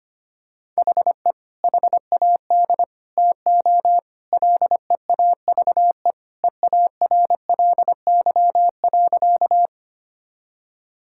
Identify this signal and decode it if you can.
Morse code, heard as a keyed tone, about 25 wpm, 700 Hz. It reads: HE HAD TO LEAVE EARLY.
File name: practice / 25wpm / Sentences with 3-Word Phrases - 25wpm